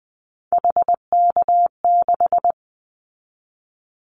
Morse code, heard as a keyed tone, about 20 wpm, 700 Hz. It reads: HK6
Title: Callsign Prefixes for Top 100 DXCC - 20wpm